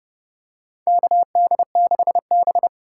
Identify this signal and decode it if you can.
Morse code, heard as a keyed tone, about 30 wpm, 700 Hz. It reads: KD6B
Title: USA Call Signs - 4 Characters - 6x Character Spacing - 30wpm